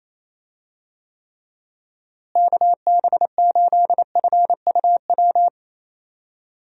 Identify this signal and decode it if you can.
Morse code, heard as a keyed tone, about 28 wpm, 700 Hz. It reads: KB8FUW
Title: USA Call Signs - 6 Characters - Repeated 3x - 28wpm